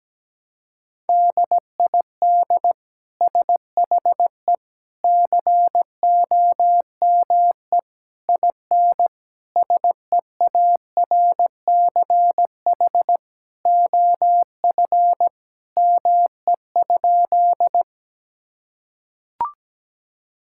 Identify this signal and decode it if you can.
Morse code, heard as a keyed tone, about 17 wpm, 700 Hz. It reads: DID SHE COME IN SEARCH OF ME?
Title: Sentences from Top 1000 Words - Review Part 5 - 17wpm